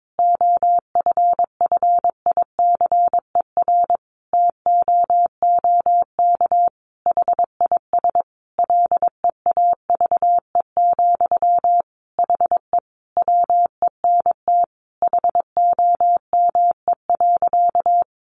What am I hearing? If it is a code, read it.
Morse code, heard as a keyed tone, about 22 wpm, 700 Hz. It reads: OFFICER TOOK HIS LEAVE, HE WENT HOME.
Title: Sentences from Top 1200 Words 22wpm